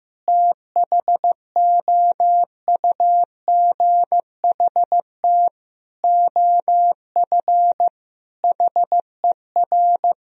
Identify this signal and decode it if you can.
Morse code, heard as a keyed tone, about 15 wpm, 700 Hz. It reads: THOUGHT OF HER
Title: Sentences from Top 1000 Words - Review Part 2 - 15wpm